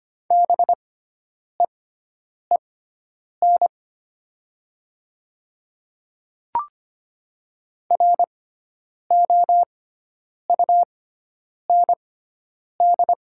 Morse code, heard as a keyed tone, about 25 wpm, spaced as for 9 wpm, 700 Hz. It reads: BEEN ROUND
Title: 6x Character Spacing - Top 200 Words - 25wpm